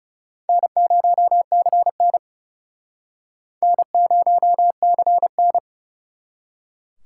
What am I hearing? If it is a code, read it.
Morse code, heard as a keyed tone, about 41 wpm, 700 Hz. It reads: N0CN N0CN